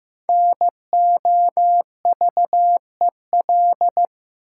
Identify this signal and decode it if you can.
Morse code, heard as a keyed tone, about 15 wpm, 700 Hz. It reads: NOVEL
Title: Sentences from Top 1300 Words - New Word Drill Alt - 15wpm